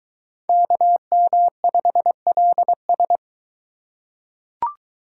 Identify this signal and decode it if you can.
Morse code, heard as a keyed tone, about 22 wpm, 700 Hz. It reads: KM5LS